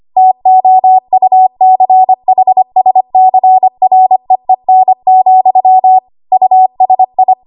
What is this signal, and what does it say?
Morse code, heard as a keyed tone, about 25 wpm, 750 Hz. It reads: TOUCHSCREEN, USI
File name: Archive / 25 WPM / 25WPM